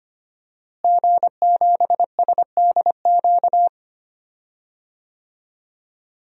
Morse code, heard as a keyed tone, about 25 wpm, 700 Hz. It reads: G7SDQ
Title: DXCC-100 Call Signs - 5 Characters - Repeated 3x - 25wpm